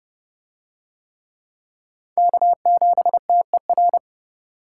Morse code, heard as a keyed tone, about 28 wpm, 700 Hz. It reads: K7TER